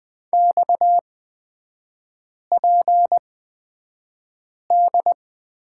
Morse code, heard as a keyed tone, about 20 wpm, spaced as for 5 wpm, 690 Hz. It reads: XPD
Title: 229 Lesson 37e - Letter x - Three Characters 20wpm